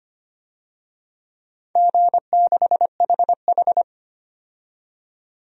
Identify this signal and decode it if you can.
Morse code, heard as a keyed tone, about 25 wpm, 700 Hz. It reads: G6HH
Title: DXCC-100 Call Signs - 4 Characters - 25wpm